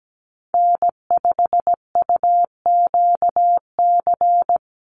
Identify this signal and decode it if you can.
Morse code, heard as a keyed tone, about 17 wpm, 700 Hz. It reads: N5UQC
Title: USA Call Signs 17wpm